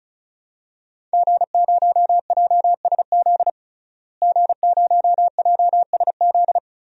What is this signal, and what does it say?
Morse code, heard as a keyed tone, about 36 wpm, 700 Hz. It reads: G0JSZ G0JSZ
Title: DXCC-100 Call Signs - 5 Characters - Repeated 2x - 35wpm